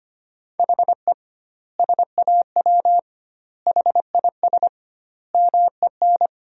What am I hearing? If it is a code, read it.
Morse code, heard as a keyed tone, about 25 wpm, 700 Hz. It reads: HE SAW HIS MEN
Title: 2x Word-Spacing - Sentences from Top 300 Words - 25wpm